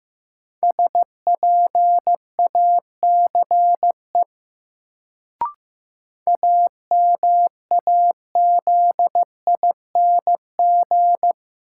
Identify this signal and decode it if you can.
Morse code, heard as a keyed tone, about 15 wpm, 700 Hz. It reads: SPACE AMAZING
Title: Sentences from Top 1500 Words - New Word Drill Alt - 15wpm